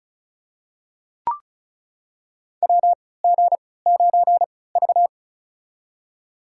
Morse code, heard as a keyed tone, about 36 wpm, spaced as for 20 wpm, 700 Hz. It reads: WG9V
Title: USA Call Signs - 3x Character Spacing - 35wpm